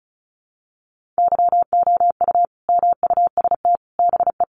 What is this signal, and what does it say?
Morse code, heard as a keyed tone, about 34 wpm, 700 Hz. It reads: YOU MUST BE
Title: Sentences from Top 1400 Words 35wpm